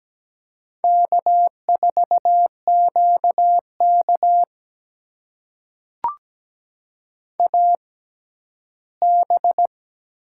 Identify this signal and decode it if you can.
Morse code, heard as a keyed tone, about 17 wpm, spaced as for 12 wpm, 700 Hz. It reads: K4QK AB
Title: USA Call Signs - 4 Characters - 6x Character Spacing - 17wpm